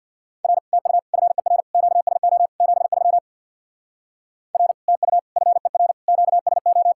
Morse code, heard as a keyed tone, about 66 wpm, 700 Hz. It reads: R TU FER QSO 73 R TU FER QSO